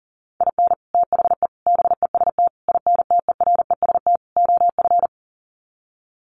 Morse code, heard as a keyed tone, about 40 wpm, 700 Hz. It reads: IN THE BEST INTEREST OF
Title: Sets of 5 Words Encore 40wpm